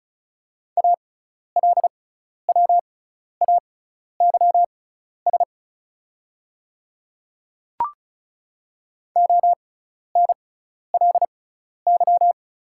Morse code, heard as a keyed tone, about 36 wpm, spaced as for 12 wpm, 690 Hz. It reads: ALWAYS ONLY
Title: 6x Character Spacing - Top 300 Words - 35wpm